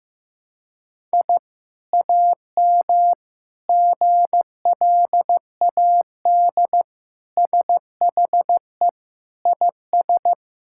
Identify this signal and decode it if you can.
Morse code, heard as a keyed tone, about 15 wpm, 700 Hz. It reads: I AM GLAD SHE IS
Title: Sentences from Top 1000 Words - Review Part 6 - 15wpm